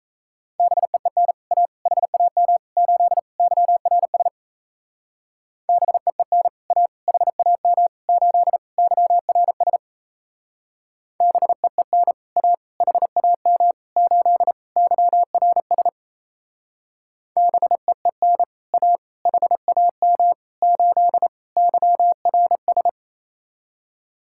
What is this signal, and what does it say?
Morse code, heard as a keyed tone, about 36 wpm, 700 Hz. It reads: BEEN A HAM 8 YRS BEEN A HAM 8 YRS BEEN A HAM 8 YRS BEEN A HAM 8 YRS